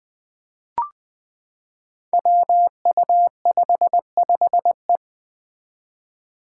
Morse code, heard as a keyed tone, about 20 wpm, 700 Hz. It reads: WU55E
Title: Five Challenging Letter-Number Combinations 20wpm